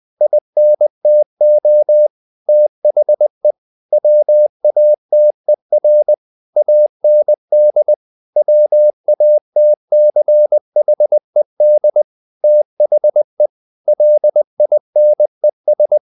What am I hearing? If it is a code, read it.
Morse code, heard as a keyed tone, about 20 wpm, 600 Hz. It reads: INTO THE WATER AND WATCHED THE LINES
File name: the old man and the sea / the old man and the sea part 1 20wpm